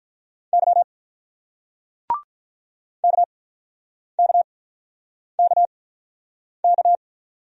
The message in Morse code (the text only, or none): Y KKKK